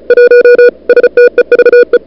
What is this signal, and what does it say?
Morse code, heard as a keyed tone, about 36 wpm, 440 Hz. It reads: 1 STEVE
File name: CWSprint / February 2007 CW Sprint / 01-2007 CW Sprint Hour1